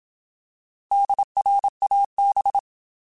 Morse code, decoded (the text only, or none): DRAB